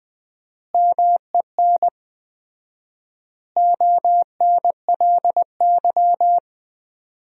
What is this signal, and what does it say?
Morse code, heard as a keyed tone, about 21 wpm, 700 Hz. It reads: MEN ONLY
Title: 4x Word-Spacing - Sentences from Top 500 Words - 20wpm